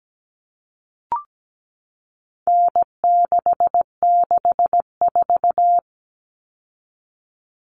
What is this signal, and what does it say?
Morse code, heard as a keyed tone, about 17 wpm, 700 Hz. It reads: N664
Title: All Cut Numbers - 4-Digit 17wpm